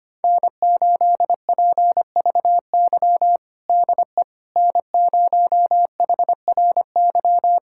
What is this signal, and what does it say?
Morse code, heard as a keyed tone, about 25 wpm, 700 Hz. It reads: N8PVY DE N0HRY
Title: USA Callsign DE Callsign - 5 Characters - 25wpm